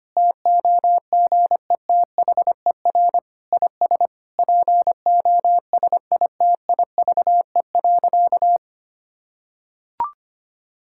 Morse code, heard as a keyed tone, about 25 wpm, 700 Hz. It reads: TOGETHER IS POSITIVE.
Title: Sentences from Top 1000 Words - Review Part 5 - 25wpm